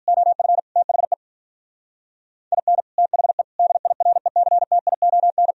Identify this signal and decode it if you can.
Morse code, heard as a keyed tone, about 42 wpm, 700 Hz. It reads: OF THE IN THE DIRECTION
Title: Speed-Racing - Sets of 5 Words 35wpm